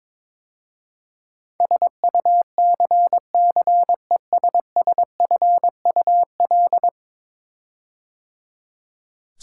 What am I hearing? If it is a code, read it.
Morse code, heard as a keyed tone, about 22 wpm, 700 Hz. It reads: SUCCESSFUL
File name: practice / 22wpm / 10 Letter Words - First 3 Letters Given - 22wpm